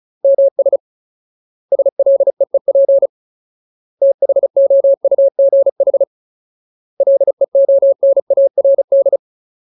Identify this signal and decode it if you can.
Morse code, heard as a keyed tone, about 36 wpm, 550 Hz. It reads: MS SLEEP THOUGH LEONARD
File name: Ultimate Word List / 35WPM Ultimate